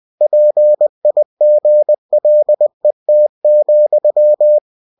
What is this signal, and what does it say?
Morse code, heard as a keyed tone, about 20 wpm, 600 Hz. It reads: PIGLET,